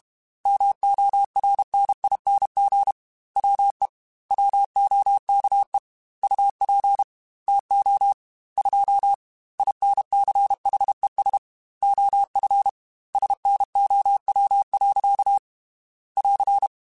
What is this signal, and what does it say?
Morse code, heard as a keyed tone, about 32 wpm, 800 Hz. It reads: MORNING WE WOKE UP TO 2 INCHES OF SNOW. <AR>
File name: Weather Sets / Weather Sets-mp3s / Weather 3 17 @30